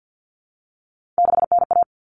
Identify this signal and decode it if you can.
Morse code, heard as a keyed tone, about 50 wpm, 700 Hz. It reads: 6NA
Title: Common Cut Numbers - 3-Digit 50wpm